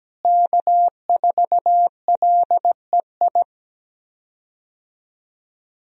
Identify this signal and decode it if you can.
Morse code, heard as a keyed tone, about 17 wpm, 700 Hz. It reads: K4LEI
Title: USA Call Signs - 5 Characters - Repeated 3x - 17wpm